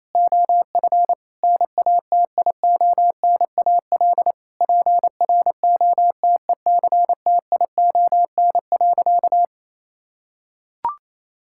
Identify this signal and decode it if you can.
Morse code, heard as a keyed tone, about 27 wpm, 700 Hz. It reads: OF NATIONAL PROTECTION.
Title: Sentences from Top 1600 Words 28wpm